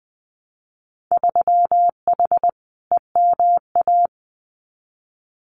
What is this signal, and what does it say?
Morse code, heard as a keyed tone, about 20 wpm, 700 Hz. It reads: 3H EMA
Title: Winter Field Day Exchanges Encore 20wpm